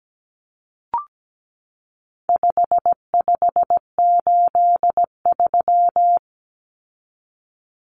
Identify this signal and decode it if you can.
Morse code, heard as a keyed tone, about 17 wpm, 700 Hz. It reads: 5583
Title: Common Cut Numbers - 4-Digit 17wpm